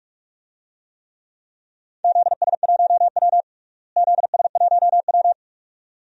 Morse code, heard as a keyed tone, about 44 wpm, 700 Hz. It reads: ZS1W ZS1W
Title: DXCC-100 Call Signs - 4 Characters - Repeated 2x - 45wpm